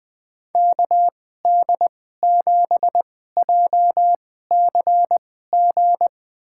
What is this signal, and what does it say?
Morse code, heard as a keyed tone, about 20 wpm, spaced as for 15 wpm, 700 Hz. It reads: KD7JCG
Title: USA Call Signs - 6 Characters - 2x Character Spacing - 20wpm